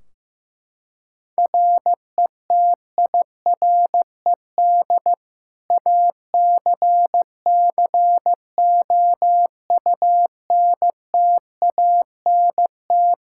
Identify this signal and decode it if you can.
Morse code, heard as a keyed tone, about 16 wpm, 700 Hz. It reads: RETIRED ACCOUNTANT
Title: Practice QSO - Exchange 3 - More Info - 15wpm